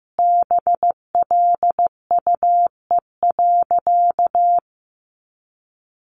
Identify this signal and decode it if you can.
Morse code, heard as a keyed tone, about 15 wpm, 700 Hz. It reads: BLUE.
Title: A Sound Of Thunder 15wpm